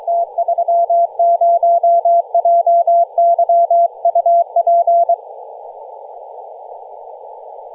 Morse code, heard as a keyed tone, about 23 wpm, 680 Hz. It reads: T30JY UP
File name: posts / 2016-nov-13-0348z-t30jy-28-034mhz-cw